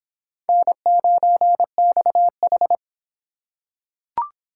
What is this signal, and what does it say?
Morse code, heard as a keyed tone, about 24 wpm, 700 Hz. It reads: N9XH